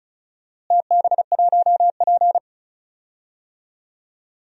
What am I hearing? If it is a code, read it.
Morse code, heard as a keyed tone, about 34 wpm, 700 Hz. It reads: TB1P
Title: DXCC-100 Call Signs - 4 Characters - Repeated 3x - 35wpm